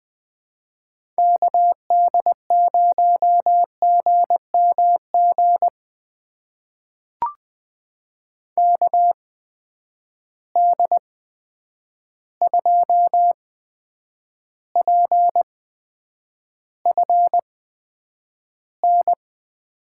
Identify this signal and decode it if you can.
Morse code, heard as a keyed tone, about 21 wpm, spaced as for 9 wpm, 700 Hz. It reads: KD0GMG KD2PFN
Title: USA Call Signs - 6 Characters - 8x Character Spacing - 20wpm